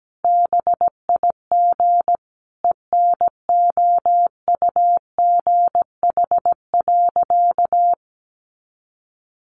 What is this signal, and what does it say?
Morse code, heard as a keyed tone, about 17 wpm, 700 Hz. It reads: BIG ENOUGH.
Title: Sentences from Top 400 Words 17wpm